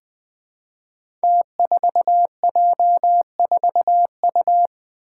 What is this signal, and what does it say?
Morse code, heard as a keyed tone, about 21 wpm, 700 Hz. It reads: T4J4U
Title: Five Challenging Letter-Number Combinations 20wpm